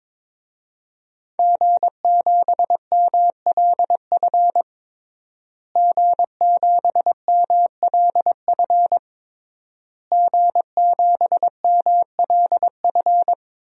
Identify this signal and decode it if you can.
Morse code, heard as a keyed tone, about 21 wpm, 700 Hz. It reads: G7MLF G7MLF G7MLF
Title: DXCC-100 Call Signs - 5 Characters - Repeated 3x - 22wpm